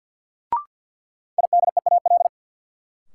Morse code, heard as a keyed tone, about 52 wpm, 700 Hz. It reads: IDEAL
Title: Sentences from Top 1400 Words - New Word Drill - 50wpm